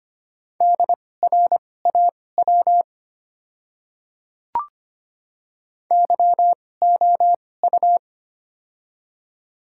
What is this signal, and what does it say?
Morse code, heard as a keyed tone, about 25 wpm, spaced as for 18 wpm, 700 Hz. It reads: DRAW YOU
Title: 2x Character Spacing - Top 300 Words - 25wpm